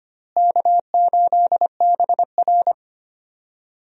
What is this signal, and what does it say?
Morse code, heard as a keyed tone, about 25 wpm, 700 Hz. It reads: K8BR
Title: USA Call Signs - 4 Characters - Repeated 3x - 25wpm